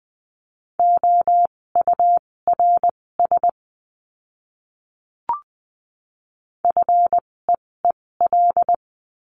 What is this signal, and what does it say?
Morse code, heard as a keyed tone, about 20 wpm, spaced as for 16 wpm, 700 Hz. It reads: OURS FEEL
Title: 256 Lesson 42a - Words 12-20wpm